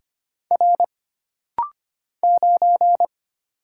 Morse code, heard as a keyed tone, about 25 wpm, 700 Hz. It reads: R 9
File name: Single Character (Letter:Number:Punctuation:Prosign) - Rapid-Fire - 25wpm